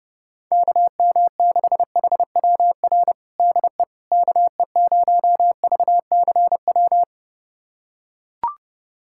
Morse code, heard as a keyed tone, about 32 wpm, 700 Hz. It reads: KM6HWR DE KE0VCW
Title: USA Callsign DE Callsign - 6 Characters - 30wpm